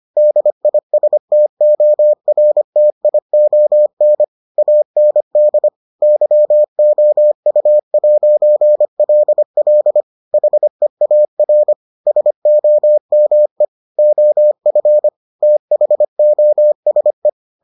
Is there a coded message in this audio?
Yes: Morse code from 0.2 to 17.3 s, heard as a keyed tone, about 25 wpm, 600 Hz. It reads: DISTORTION AND YOU'LL HEAR SOME OF THOSE